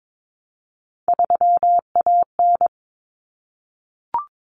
Morse code, heard as a keyed tone, about 22 wpm, 700 Hz. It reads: 3AN
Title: Common Cut Numbers - 3-Digit 22wpm